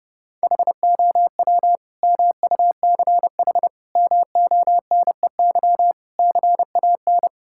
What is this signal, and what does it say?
Morse code, heard as a keyed tone, about 30 wpm, 700 Hz. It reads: HOW MUCH MONEY CAN
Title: Sentences from Top 1000 Words - Review Part 5 - 30wpm